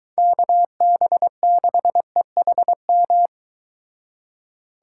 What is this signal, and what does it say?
Morse code, heard as a keyed tone, about 22 wpm, 700 Hz. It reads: KB6EHM